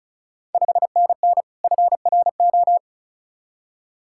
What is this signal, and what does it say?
Morse code, heard as a keyed tone, about 36 wpm, 690 Hz. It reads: 5NN FRO
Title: Ontario QSO Party Exchange 35wpm